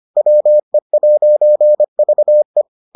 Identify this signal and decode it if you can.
Morse code, heard as a keyed tone, about 25 wpm, 600 Hz. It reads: WE'VE